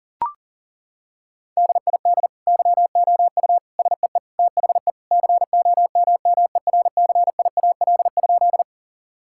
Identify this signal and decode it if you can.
Morse code, heard as a keyed tone, about 37 wpm, 700 Hz. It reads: DID YOU SEE THE COMMERCIAL?
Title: Sentences from Top 1000 Words - Review Part 5 - 40wpm